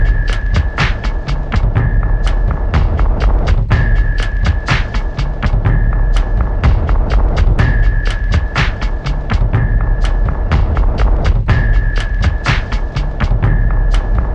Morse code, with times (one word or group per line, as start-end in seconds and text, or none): none